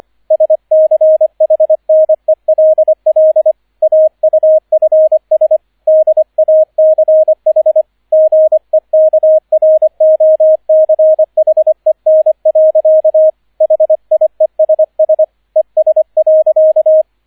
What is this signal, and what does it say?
Morse code, heard as a keyed tone, about 25 wpm, 630 Hz, with radio noise. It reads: SCHNELL AUFS DACH GEKROCHEN. HIESS ES.